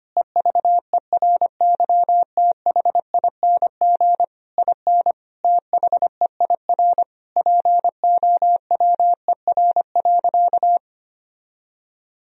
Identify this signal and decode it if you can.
Morse code, heard as a keyed tone, about 25 wpm, 700 Hz. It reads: EVERYTHING IN THEIR POWER.
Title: Sentences from Top 1000 Words - Review Part 2 - 25wpm